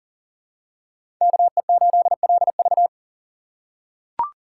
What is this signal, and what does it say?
Morse code, heard as a keyed tone, about 40 wpm, 700 Hz. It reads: KE8LV
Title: USA Call Signs - 5 Characters - 3x Character Spacing - 40wpm